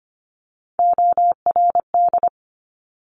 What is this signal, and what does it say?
Morse code, heard as a keyed tone, about 25 wpm, 700 Hz. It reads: ORD
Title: Top Three Letter Combinations in Words 25wpm